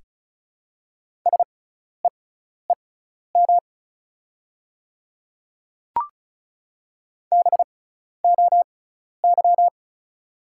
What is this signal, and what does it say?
Morse code, heard as a keyed tone, about 36 wpm, spaced as for 12 wpm, 700 Hz. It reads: SEEM BOY
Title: 6x Character Spacing - Top 300 Words - 35wpm